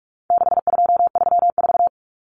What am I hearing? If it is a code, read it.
Morse code, heard as a keyed tone, about 44 wpm, 700 Hz. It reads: 6234
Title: Common Cut Numbers - 4-Digit 45wpm